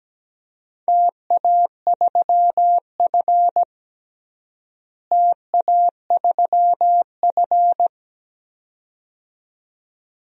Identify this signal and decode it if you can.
Morse code, heard as a keyed tone, about 17 wpm, 700 Hz. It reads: TA3F TA3F
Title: DXCC-100 Call Signs - 4 Characters - Repeated 2x - 17wpm